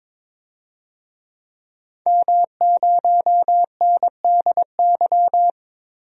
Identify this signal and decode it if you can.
Morse code, heard as a keyed tone, about 22 wpm, 700 Hz. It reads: M0NDY
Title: DXCC-100 Call Signs - 5 Characters - Repeated 3x - 22wpm